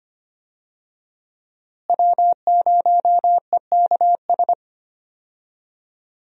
Morse code, heard as a keyed tone, about 25 wpm, 700 Hz. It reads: W0EKS
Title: USA Call Signs - 5 Characters - Repeated 3x - 25wpm